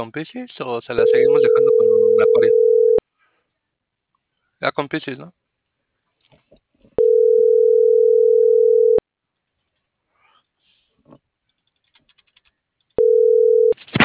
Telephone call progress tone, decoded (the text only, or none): ringback tone